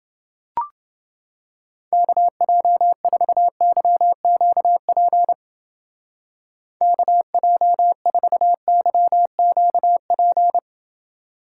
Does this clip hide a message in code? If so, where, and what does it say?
Yes: Morse code from 1.9 to 10.6 s, heard as a keyed tone, about 30 wpm, 700 Hz. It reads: KJ4YQP KJ4YQP